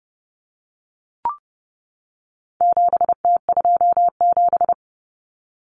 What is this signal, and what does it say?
Morse code, heard as a keyed tone, about 30 wpm, 700 Hz. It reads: 7T27
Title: Common Cut Numbers - 4-Digit 30wpm